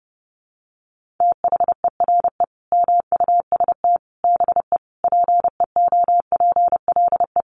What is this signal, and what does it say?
Morse code, heard as a keyed tone, about 30 wpm, 700 Hz. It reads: THERE MUST BE PEOPLE
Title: Sentences from Top 1400 Words 30wpm